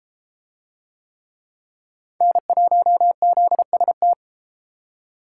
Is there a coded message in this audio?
Yes: Morse code from 2.2 to 4.1 s, heard as a keyed tone, about 29 wpm, 700 Hz. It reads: N1ZST